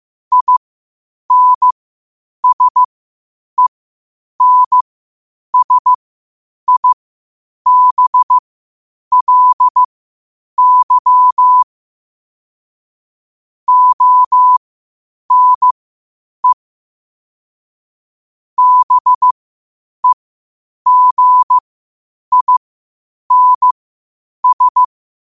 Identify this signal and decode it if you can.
Morse code, heard as a keyed tone, about 15 wpm, spaced as for 8 wpm, 1000 Hz. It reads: INSENSIBLY ONE BEGINS
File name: qotdfiles / 8wpm